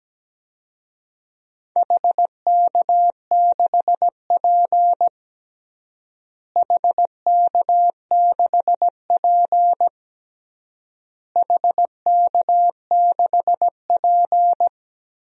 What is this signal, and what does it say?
Morse code, heard as a keyed tone, about 17 wpm, 700 Hz. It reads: HK6P HK6P HK6P
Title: DXCC-100 Call Signs - 4 Characters - Repeated 3x - 17wpm